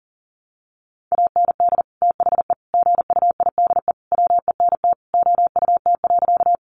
Morse code, heard as a keyed tone, about 40 wpm, 700 Hz. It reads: AND THE GUIDE WENT OUT.
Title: Sentences from Top 600 Words 40wpm